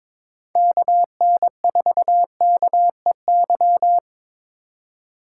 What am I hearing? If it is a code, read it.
Morse code, heard as a keyed tone, about 26 wpm, 700 Hz. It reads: KN4KEY
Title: Speed Racing - USA 2x3 Call Signs 22wpm